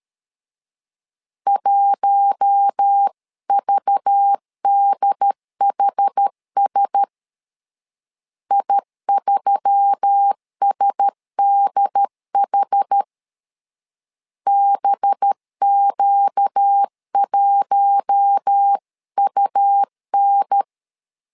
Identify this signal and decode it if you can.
Morse code, heard as a keyed tone, about 13 wpm, 800 Hz. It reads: 1VDHS I3SDH BQ1UN
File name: LEZIONI / 5 LBVC / 5 LBVC 12wpm 1955